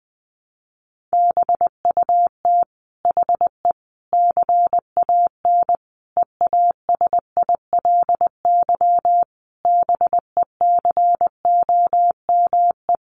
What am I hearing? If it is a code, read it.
Morse code, heard as a keyed tone, about 20 wpm, 700 Hz. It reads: BUT HE CAN EASILY BECOME
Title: Sentences from Top 900 Words 20wpm